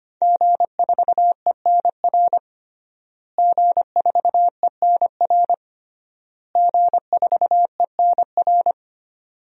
Morse code, heard as a keyed tone, about 25 wpm, 700 Hz. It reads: G4ENR G4ENR G4ENR
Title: DXCC-100 Call Signs - 5 Characters - Repeated 3x - 25wpm